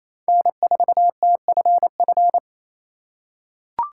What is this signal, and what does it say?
Morse code, heard as a keyed tone, about 28 wpm, 700 Hz. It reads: N4TFF
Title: USA Call Signs - 3x Character Spacing - 28wpm